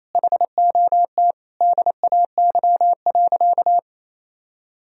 Morse code, heard as a keyed tone, about 27 wpm, 700 Hz. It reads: HOT DAY.
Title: Sentences from Top 1000 Words - Review Part 3 - 28wpm